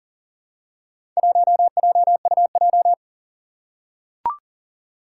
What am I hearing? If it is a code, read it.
Morse code, heard as a keyed tone, about 41 wpm, 700 Hz. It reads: 1JUJ
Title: Four Challenging Letter-Number Combinations 40wpm